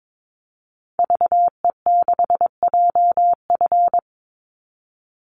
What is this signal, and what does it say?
Morse code, heard as a keyed tone, about 22 wpm, 700 Hz. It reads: VE6JF
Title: Call Signs - 22wpm